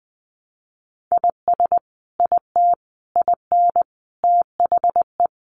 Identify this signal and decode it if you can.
Morse code, heard as a keyed tone, about 20 wpm, 700 Hz. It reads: IS IT IN THE
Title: Sentences from Top 1300 Words 20wpm